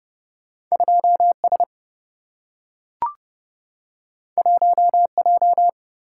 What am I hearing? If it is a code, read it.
Morse code, heard as a keyed tone, about 30 wpm, 700 Hz. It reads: 2 S 1 J
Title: Single-Letter Single-Number 30wpm